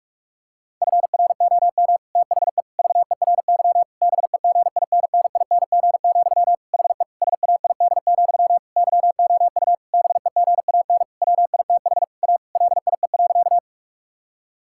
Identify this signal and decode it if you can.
Morse code, heard as a keyed tone, about 46 wpm, 700 Hz. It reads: FROM THE VERY BEGINNING, HE SAID, YOU BEGAN WITH A LIE.
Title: Sentences from Top 1600 Words 45wpm